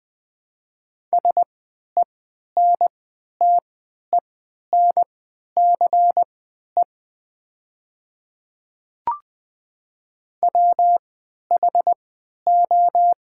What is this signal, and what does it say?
Morse code, heard as a keyed tone, about 20 wpm, spaced as for 11 wpm, 700 Hz. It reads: SENTENCE WHO
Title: 3x Character Spacing - Top 200 Words - 20wpm